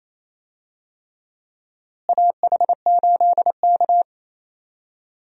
Morse code, heard as a keyed tone, about 28 wpm, 700 Hz. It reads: AH8K
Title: USA Call Signs - 4 Characters - Repeated 3x - 28wpm